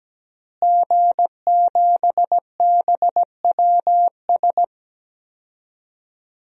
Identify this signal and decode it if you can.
Morse code, heard as a keyed tone, about 17 wpm, 700 Hz. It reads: G7BWS
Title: DXCC-100 Call Signs - 5 Characters - Repeated 3x - 17wpm